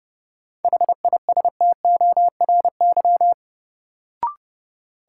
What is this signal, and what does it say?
Morse code, heard as a keyed tone, about 30 wpm, 700 Hz. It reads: HISTORY
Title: Sentences from Top 600 Words - New Word Drill - 30wpm